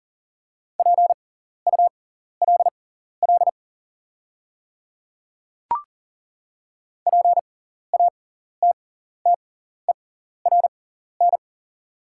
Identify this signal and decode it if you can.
Morse code, heard as a keyed tone, about 40 wpm, spaced as for 14 wpm, 690 Hz. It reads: PULL PATTERN
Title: 6x Character Spacing - Top 500 Words - 40wpm